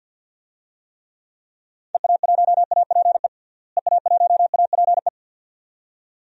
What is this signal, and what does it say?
Morse code, heard as a keyed tone, about 51 wpm, 700 Hz. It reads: EA1APE EA1APE
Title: DXCC-100 Call Signs - 6 Characters - Repeated 2x - 50wpm